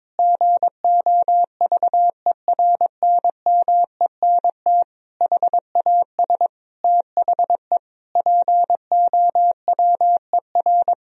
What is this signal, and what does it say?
Morse code, heard as a keyed tone, about 22 wpm, 700 Hz. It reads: GOVERNMENT HAS THE POWER
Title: Sentences from Top 1000 Words - Review Part 3 - 22wpm